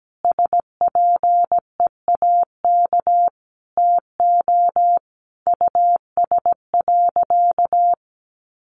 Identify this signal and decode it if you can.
Morse code, heard as a keyed tone, about 17 wpm, 700 Hz. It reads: SPEAK TO US.
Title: Sentences from Top 1400 Words 17wpm